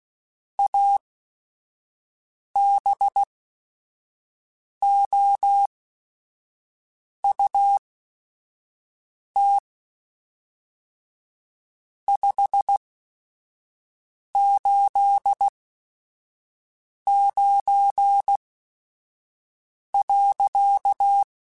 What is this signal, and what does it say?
Morse code, heard as a keyed tone, about 16 wpm, spaced as for 5 wpm, 800 Hz. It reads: ABOUT 589.
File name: QSOs / QSOs-mp3s / WD0FBT 1@5